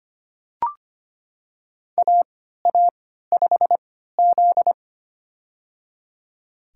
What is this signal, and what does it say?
Morse code, heard as a keyed tone, about 25 wpm, spaced as for 14 wpm, 700 Hz. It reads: AA5Z
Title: USA Call Signs - 3x Character Spacing - 25wpm